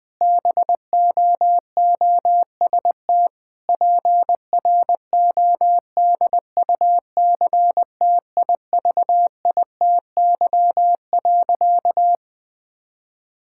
Morse code, heard as a keyed tone, about 20 wpm, 700 Hz. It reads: BOOST PRODUCTIVITY.
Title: Difficult Sentences with Hints - 20wpm